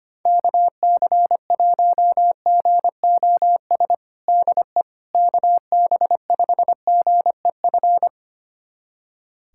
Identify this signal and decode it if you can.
Morse code, heard as a keyed tone, about 25 wpm, 700 Hz. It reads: KC1GOS DE KB5GEF
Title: USA Callsign DE Callsign - 6 Characters - 25wpm